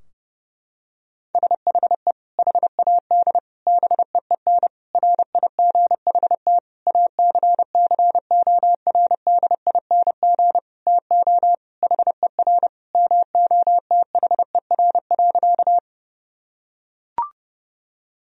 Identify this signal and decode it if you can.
Morse code, heard as a keyed tone, about 32 wpm, 700 Hz. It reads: SHE HAD BEEN RIGHT ACCORDING TO HER MOTHER.
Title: Sentences from Top 1000 Words - Review Part 4 - 30wpm